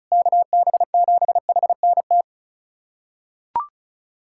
Morse code, heard as a keyed tone, about 35 wpm, 700 Hz. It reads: KB7HNT
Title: USA Callsign DE Callsign - 6 Characters - 35wpm